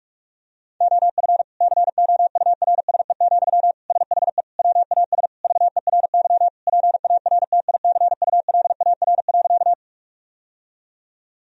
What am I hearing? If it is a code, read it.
Morse code, heard as a keyed tone, about 41 wpm, 700 Hz. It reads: OF COURSE, SHE WAS VERY PARTICULAR.
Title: Sentences from Top 1000 Words - Review Part 4 - 45wpm